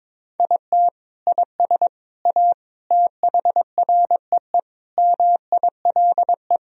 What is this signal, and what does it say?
Morse code, heard as a keyed tone, about 22 wpm, 700 Hz. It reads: IT IS A THREE MILE
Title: Sentences from Top 1000 Words - Review Part 1 - 22wpm